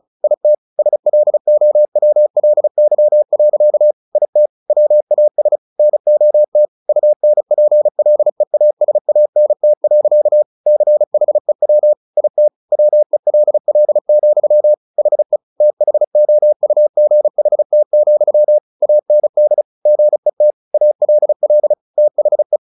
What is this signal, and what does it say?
Morse code, heard as a keyed tone, about 34 wpm, 600 Hz. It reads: IT SLOWLY. IT WAS NOT UNPLEASANT. CHEW IT WELL, HE THOUGHT, AND GET ALL THE
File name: the old man and the sea / the old man and the sea 1 35wpm